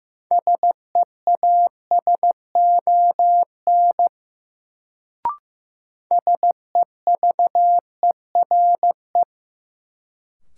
Morse code, heard as a keyed tone, about 15 wpm, 700 Hz. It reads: SEASON SEVERE
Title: Sentences from Top 1400 Words - New Word Drill Alt - 15wpm